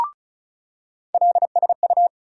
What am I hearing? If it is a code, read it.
Morse code, heard as a keyed tone, about 36 wpm, 700 Hz. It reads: LSU
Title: USA Call Sign Postfixes - 3 Letters 35wpm